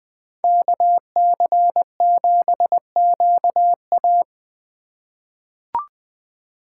Morse code, heard as a keyed tone, about 20 wpm, 700 Hz. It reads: KC7QA
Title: USA Call Signs - 5 Characters - 4x Character Spacing - 20wpm